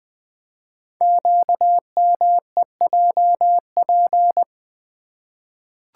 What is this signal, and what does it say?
Morse code, heard as a keyed tone, about 20 wpm, 700 Hz. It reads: QMEJP
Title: Code Groups - 5 Letter:Numbers - 20wpm